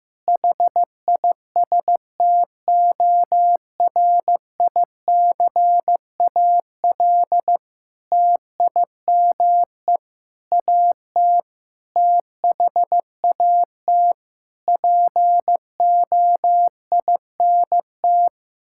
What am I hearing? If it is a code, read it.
Morse code, heard as a keyed tone, about 15 wpm, 700 Hz. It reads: HISTORICAL TIME AT THAT POINT
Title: Sentences from Top 1000 Words - Review Part 5 - 15wpm